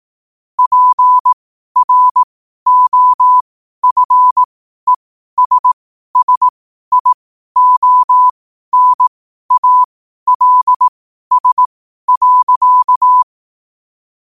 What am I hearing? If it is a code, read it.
Morse code, heard as a keyed tone, about 18 wpm, spaced as for 12 wpm, 1000 Hz. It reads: PROFESSIONALS.
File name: qotdfiles / 12wpm